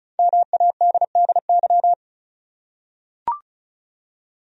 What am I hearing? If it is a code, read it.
Morse code, heard as a keyed tone, about 36 wpm, 700 Hz. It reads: MADDY
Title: USA Common Names - 35wpm